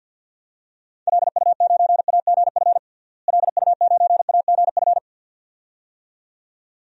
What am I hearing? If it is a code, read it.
Morse code, heard as a keyed tone, about 52 wpm, 700 Hz. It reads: LU9AGF LU9AGF
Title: DXCC-100 Call Signs - 6 Characters - Repeated 2x - 50wpm